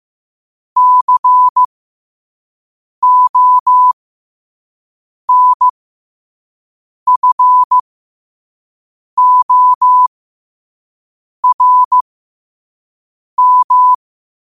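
Morse code, heard as a keyed tone, about 15 wpm, spaced as for 5 wpm, 1000 Hz. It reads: CONFORM
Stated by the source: Morse code sent at 5 wpm.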